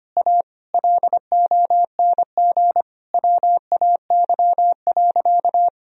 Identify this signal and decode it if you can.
Morse code, heard as a keyed tone, about 25 wpm, 700 Hz. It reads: A LONG WAY.
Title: Sentences from Top 1000 Words - Review Part 1 - 25wpm